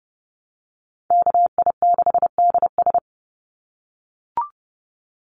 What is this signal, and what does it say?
Morse code, heard as a keyed tone, about 30 wpm, 700 Hz. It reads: KI6DS
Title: NA SOTA Chaser Callsigns 30wpm